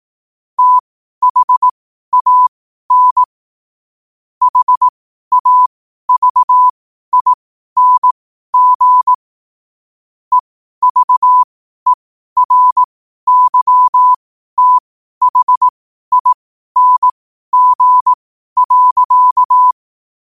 Morse code, heard as a keyed tone, about 18 wpm, spaced as for 12 wpm, 1000 Hz. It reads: THAN HAVING EVERYTHING.